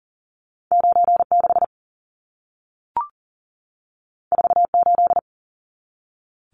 Morse code, heard as a keyed tone, about 40 wpm, 700 Hz. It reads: 96 48
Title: Numbers - 2-Digit 40wpm